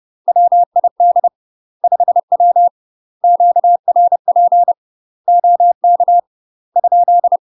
Morse code, heard as a keyed tone, about 30 wpm, 700 Hz. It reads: WID 5W QRP OK ?
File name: UK MASTER QSOs TRAINING / 30WPM / UK MASTER QSOs 0039